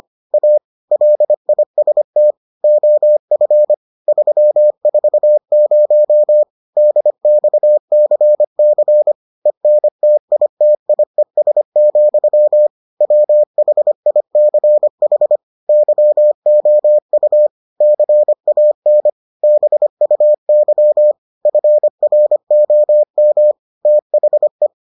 Morse code, heard as a keyed tone, about 25 wpm, 600 Hz. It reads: A LIST OF 340 DXCC ENTITIES, WHICH YOU CAN BUY FROM THE